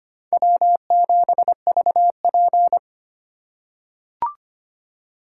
Morse code, heard as a keyed tone, about 25 wpm, 700 Hz. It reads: W7VP
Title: K1USN SST QSO Practice - 25wpm